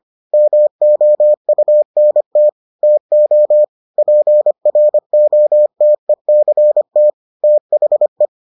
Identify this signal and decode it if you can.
Morse code, heard as a keyed tone, about 25 wpm, 600 Hz. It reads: MOUNT TO PROTECT THE